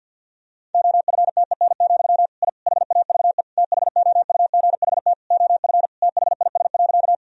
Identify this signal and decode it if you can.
Morse code, heard as a keyed tone, about 46 wpm, 700 Hz. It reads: OFTEN, I HAVE THOUGHT OF THIS.
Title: Sentences from Top 1000 Words - Review Part 1 - 50wpm